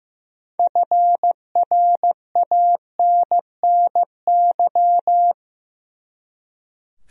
Morse code, heard as a keyed tone, about 15 wpm, 700 Hz. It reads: FRANNY
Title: USA Common Names - 15wpm